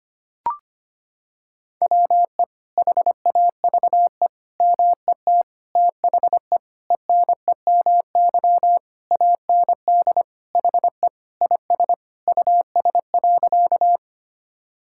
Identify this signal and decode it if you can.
Morse code, heard as a keyed tone, about 25 wpm, 700 Hz. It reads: WE HAVE MET THE ENEMY AND HE IS US.
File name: practice / 25wpm / Sentences with 9-Word Phrases - 25wpm